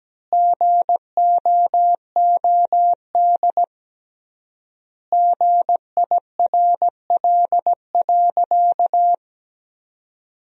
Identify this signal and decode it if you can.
Morse code, heard as a keyed tone, about 17 wpm, 700 Hz. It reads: GOOD GIRL.
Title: 3x Word-Spacing - Sentences from Top 400 Words - 17wpm